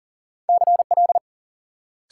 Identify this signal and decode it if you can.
Morse code, heard as a keyed tone, about 53 wpm, 700 Hz. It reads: CL